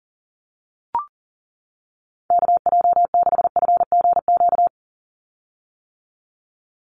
Morse code, heard as a keyed tone, about 40 wpm, 700 Hz. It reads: KJ6FGQ
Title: USA Call Signs 40wpm